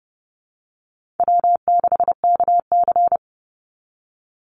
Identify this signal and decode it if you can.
Morse code, heard as a keyed tone, about 30 wpm, 700 Hz. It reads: W6KC
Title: Easy Call Signs Repeated 3x - 30wpm